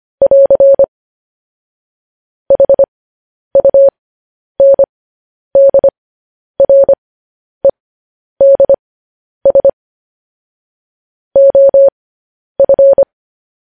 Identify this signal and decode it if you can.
Morse code, heard as a keyed tone, about 25 wpm, spaced as for 10 wpm, 550 Hz. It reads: <AR> HUNDREDS OF
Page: news-headlines-in-morse-code-at-10-wpm